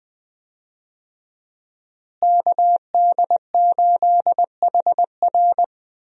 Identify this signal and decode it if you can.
Morse code, heard as a keyed tone, about 21 wpm, 700 Hz. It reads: KD8HR
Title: USA Call Signs - 5 Characters - Repeated 3x - 20wpm